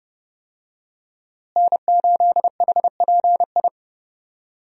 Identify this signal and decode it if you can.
Morse code, heard as a keyed tone, about 31 wpm, 700 Hz. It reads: N8HPI
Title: USA Call Signs - 5 Characters - Repeated 2x - 30wpm